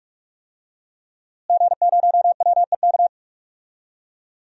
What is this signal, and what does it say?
Morse code, heard as a keyed tone, about 47 wpm, 700 Hz. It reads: G0WEK
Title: DXCC-100 Call Signs - 5 Characters - Repeated 2x - 45wpm